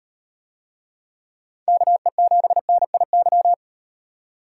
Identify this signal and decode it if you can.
Morse code, heard as a keyed tone, about 36 wpm, 700 Hz. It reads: KE7NIY